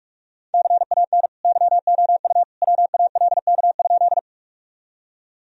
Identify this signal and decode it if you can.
Morse code, heard as a keyed tone, about 46 wpm, 700 Hz. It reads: CAN YOU WALK?
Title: Sentences from Top 1000 Words - Review Part 1 - 45wpm